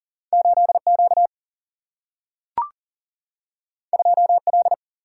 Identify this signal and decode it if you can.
Morse code, heard as a keyed tone, about 40 wpm, 700 Hz. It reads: 8 Q 2 L
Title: Single-Letter Single-Number 40wpm